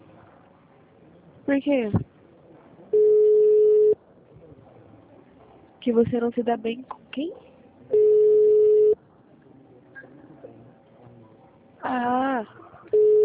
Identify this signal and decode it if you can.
Ringback tone (Europe, 425 Hz), with speech.
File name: out- -2013- - - .4